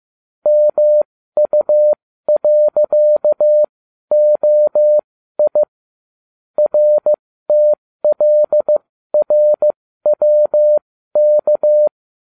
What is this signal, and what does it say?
Morse code, heard as a keyed tone, about 15 wpm, spaced as for 12 wpm, 600 Hz. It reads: MU.OI RTLRWK